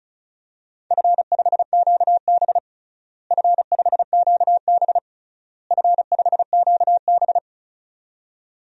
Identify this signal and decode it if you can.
Morse code, heard as a keyed tone, about 34 wpm, 700 Hz. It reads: F5QB F5QB F5QB
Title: DXCC-100 Call Signs - 4 Characters - Repeated 3x - 35wpm